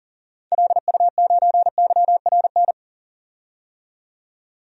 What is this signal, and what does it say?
Morse code, heard as a keyed tone, about 40 wpm, 700 Hz. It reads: LU9YRN
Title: DXCC-100 Call Signs - 6 Characters - 40wpm